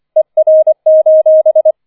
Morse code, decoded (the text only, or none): ER: